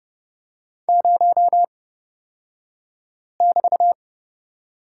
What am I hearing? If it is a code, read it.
Morse code, heard as a keyed tone, about 30 wpm, spaced as for 5 wpm, 700 Hz. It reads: 0<BT>
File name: Lesson 37d - Letter x - Two Characters 30wpm